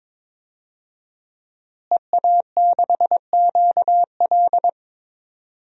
Morse code, heard as a keyed tone, about 22 wpm, 700 Hz. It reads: EA6QL
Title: DXCC-100 Call Signs - 5 Characters - Repeated 2x - 22wpm